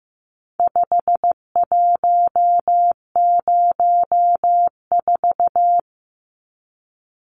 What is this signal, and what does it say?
Morse code, heard as a keyed tone, about 15 wpm, 700 Hz. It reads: 5104
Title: Numbers - 4-Digit 15wpm